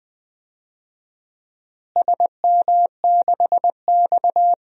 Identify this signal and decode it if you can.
Morse code, heard as a keyed tone, about 20 wpm, 700 Hz. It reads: SM6X
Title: DXCC-100 Call Signs - 4 Characters - Repeated 2x - 20wpm